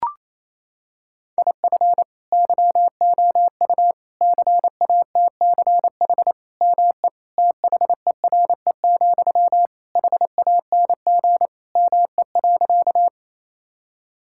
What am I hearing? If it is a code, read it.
Morse code, heard as a keyed tone, about 28 wpm, 700 Hz. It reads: IF YOU CATCH ME THERE, HANG ME.
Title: Sentences from Top 1600 Words 28wpm